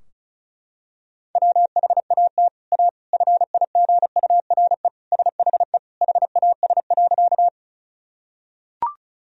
Morse code, heard as a keyed tone, about 36 wpm, 700 Hz. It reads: WHAT A FIGURE SHE HAS.
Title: Sentences from Top 1000 Words - Review Part 2 - 35wpm